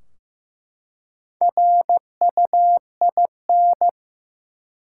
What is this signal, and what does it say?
Morse code, heard as a keyed tone, about 15 wpm, 700 Hz. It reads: RUIN
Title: Four Letter Words - 15wpm